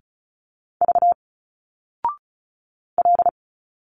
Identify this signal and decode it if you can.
Morse code, heard as a keyed tone, about 36 wpm, 700 Hz. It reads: V L
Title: Single Letters 35wpm